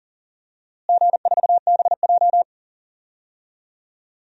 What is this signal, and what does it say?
Morse code, heard as a keyed tone, about 40 wpm, 700 Hz. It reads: G4BJ
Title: DXCC-100 Call Signs - 4 Characters - Repeated 2x - 40wpm